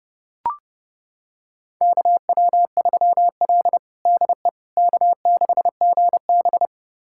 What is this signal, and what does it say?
Morse code, heard as a keyed tone, about 32 wpm, 700 Hz. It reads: KW3L DE K6GB
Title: USA Callsign DE Callsign - 4 Characters - 30wpm